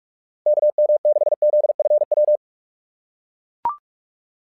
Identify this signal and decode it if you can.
Morse code, heard as a keyed tone, about 44 wpm, 600 Hz. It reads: KM6ZFW
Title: SOTA QSO Practice 45wpm